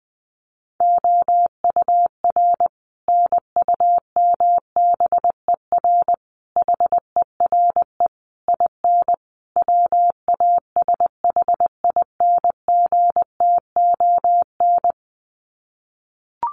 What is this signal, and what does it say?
Morse code, heard as a keyed tone, about 20 wpm, 700 Hz. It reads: OUR NUMBER HERE IN WASHINGTON
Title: Sets of 5 Words Encore2 20wpm